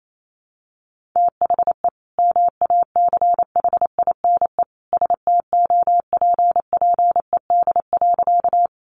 Morse code, heard as a keyed tone, about 28 wpm, 700 Hz. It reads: THE MACHINE STOPPED.
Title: A Sound Of Thunder 28wpm